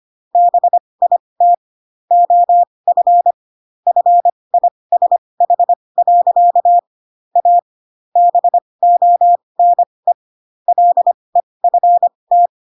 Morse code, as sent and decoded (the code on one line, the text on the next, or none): -... .. - / --- ..-. / ..-. .. ... .... .-.-.- / .- / -... --- -. / .-.. ..-. -
BIT OF FISH. A BONE LEFT